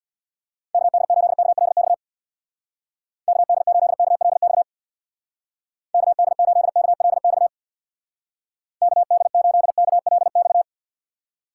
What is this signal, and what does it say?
Morse code, heard as a keyed tone, about 63 wpm, 700 Hz. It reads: KD8KLX KD8KLX KD8KLX KD8KLX